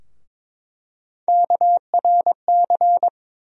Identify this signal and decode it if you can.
Morse code, heard as a keyed tone, about 22 wpm, 700 Hz. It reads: KRC
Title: USA Call Sign Postfixes - 3 Letters 22wpm